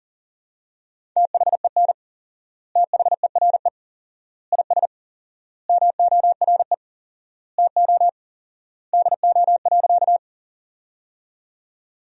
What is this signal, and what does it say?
Morse code, heard as a keyed tone, about 37 wpm, 700 Hz. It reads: THEN THERE IS MORE TO DO.
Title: 4x Word-Spacing - Sentences from Top 100 Words - 40wpm